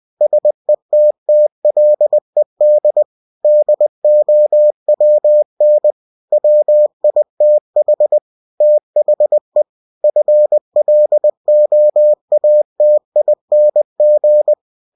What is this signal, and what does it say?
Morse code, heard as a keyed tone, about 20 wpm, 600 Hz. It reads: SETTLED DOWN WITH THE FLOATING
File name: cw-books / winnie-the-pooh / winnie-20wpm-page-0095